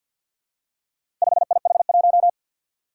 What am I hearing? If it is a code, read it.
Morse code, heard as a keyed tone, about 52 wpm, 700 Hz. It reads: 5IH1
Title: Four Challenging Letter-Number Combinations 50wpm